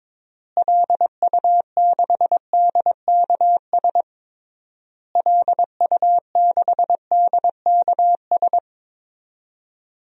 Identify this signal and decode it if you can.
Morse code, heard as a keyed tone, about 22 wpm, 700 Hz. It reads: LU6DKS LU6DKS
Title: DXCC-100 Call Signs - 6 Characters - Repeated 3x - 22wpm